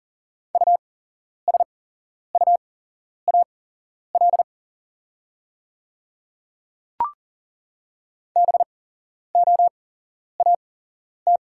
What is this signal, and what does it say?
Morse code, heard as a keyed tone, about 40 wpm, spaced as for 11 wpm, 690 Hz. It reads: USUAL BOAT
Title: 8x Character Spacing - Top 500 Words - 40wpm